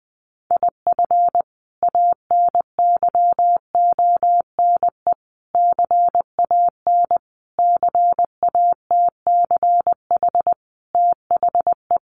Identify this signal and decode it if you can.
Morse code, heard as a keyed tone, about 20 wpm, 700 Hz. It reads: IF ANYONE CAN CATCH THE
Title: Sentences from Top 1500 Words 20wpm